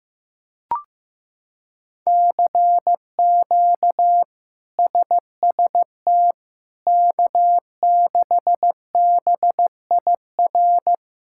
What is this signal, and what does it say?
Morse code, heard as a keyed tone, about 15 wpm, 700 Hz. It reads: CQ SST K6BIR
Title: K1USN SST QSO Practice - 15wpm